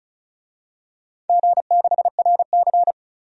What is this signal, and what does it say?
Morse code, heard as a keyed tone, about 34 wpm, 700 Hz. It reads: G6RC
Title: DXCC-100 Call Signs - 4 Characters - 35wpm